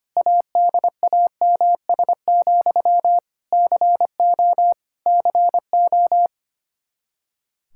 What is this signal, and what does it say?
Morse code, heard as a keyed tone, about 25 wpm, 700 Hz. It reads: ADAMS, CO CO
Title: Practice QSO - Exchange 1 - Beginning - 25wpm